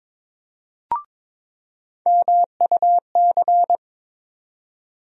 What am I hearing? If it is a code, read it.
Morse code, heard as a keyed tone, about 22 wpm, 700 Hz. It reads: MUC
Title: USA Call Sign Postfixes - 3 Letters 22wpm